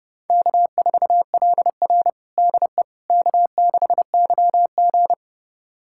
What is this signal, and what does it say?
Morse code, heard as a keyed tone, about 30 wpm, 700 Hz. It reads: K4LR DE K6YG
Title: USA Callsign DE Callsign - 4 Characters - 30wpm